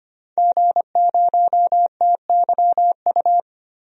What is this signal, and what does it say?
Morse code, heard as a keyed tone, about 25 wpm, 700 Hz. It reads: G0TYU
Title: DXCC-100 Call Signs - 5 Characters - Repeated 2x - 25wpm